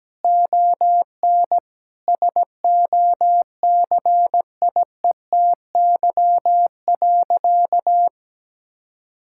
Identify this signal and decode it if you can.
Morse code, heard as a keyed tone, about 17 wpm, 700 Hz. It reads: ON SOCIETY.
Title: Sentences from Top 1000 Words - Review Part 6 - 17wpm